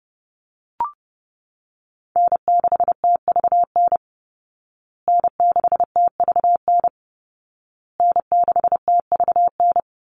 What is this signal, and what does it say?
Morse code, heard as a keyed tone, about 30 wpm, 700 Hz. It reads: N6TVN N6TVN N6TVN
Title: Call Signs Repeated 2x - 30wpm